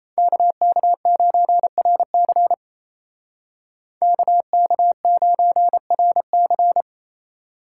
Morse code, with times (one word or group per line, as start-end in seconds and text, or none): 0.2-2.5 KK9RC
4.0-6.8 KK9RC